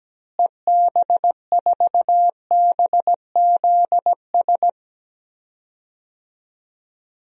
Morse code, heard as a keyed tone, about 17 wpm, 700 Hz. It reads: EB4BZS
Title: DXCC-100 Call Signs - 6 Characters - Repeated 3x - 17wpm